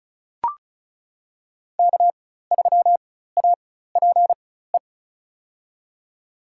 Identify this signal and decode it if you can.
Morse code, heard as a keyed tone, about 36 wpm, spaced as for 16 wpm, 700 Hz. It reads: K3APE
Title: USA Call Signs - 5 Characters - 4x Character Spacing - 35wpm